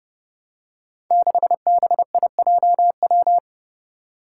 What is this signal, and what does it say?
Morse code, heard as a keyed tone, about 30 wpm, 700 Hz. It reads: 6BIJW
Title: Five Challenging Letter-Number Combinations 30wpm